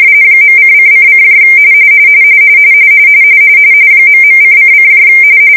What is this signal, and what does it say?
RTTY, about 45.45 baud. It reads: .:::: ..:::